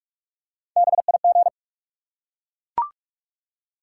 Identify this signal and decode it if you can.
Morse code, heard as a keyed tone, about 42 wpm, 700 Hz. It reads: BIG
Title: Three Letter Words - 45wpm